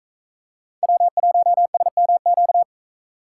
Morse code, heard as a keyed tone, about 41 wpm, 700 Hz. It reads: W1SMQ